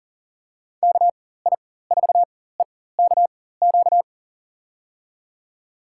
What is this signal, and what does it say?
Morse code, heard as a keyed tone, about 40 wpm, spaced as for 19 wpm, 700 Hz. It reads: KI4EKQ
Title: USA Call Signs - 6 Characters - 4x Character Spacing - 40wpm